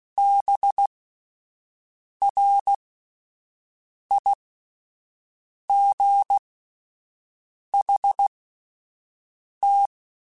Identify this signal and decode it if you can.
Morse code, heard as a keyed tone, about 16 wpm, spaced as for 5 wpm, 800 Hz. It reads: BRIGHT
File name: Spaced Characters / Spaced Characters-mp3s / Weather5 5 sp @10